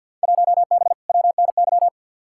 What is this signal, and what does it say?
Morse code, heard as a keyed tone, about 51 wpm, 700 Hz. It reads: 1B WNY
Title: Field Day Exchanges - 50wpm